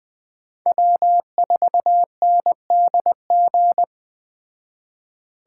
Morse code, heard as a keyed tone, about 21 wpm, 700 Hz. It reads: W4NDG
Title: Speed Racing - USA 1x3 Call Signs 17wpm